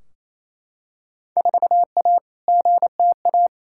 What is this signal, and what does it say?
Morse code, heard as a keyed tone, about 28 wpm, 700 Hz. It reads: 4A GTA
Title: Field Day Exchanges Encore - 28wpm